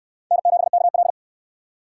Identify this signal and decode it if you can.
Morse code, heard as a keyed tone, about 66 wpm, 700 Hz. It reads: N7KL